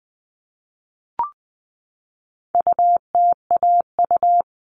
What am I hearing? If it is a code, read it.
Morse code, heard as a keyed tone, about 20 wpm, 700 Hz. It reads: UTAU
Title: All Cut Numbers - 4-Digit 20wpm